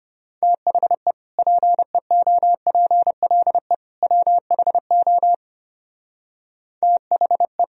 Morse code, heard as a keyed tone, about 30 wpm, 700 Hz. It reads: THE PEOPLE WHO THE
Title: Speed-Racing - Sets of 3 Words 25wpm